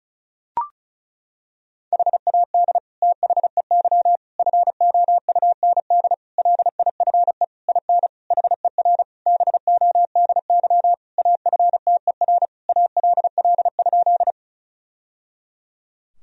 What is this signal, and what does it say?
Morse code, heard as a keyed tone, about 36 wpm, 700 Hz. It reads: HAD THEY FOUND LIFE IN HER BODY AFTER ALL?
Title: Sentences from Top 1000 Words - Review Part 2 - 35wpm